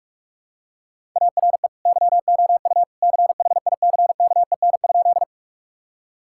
Morse code, heard as a keyed tone, about 45 wpm, 700 Hz. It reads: ARE YOU CHICKEN?
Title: Sentences from Top 1000 Words - Review Part 5 - 45wpm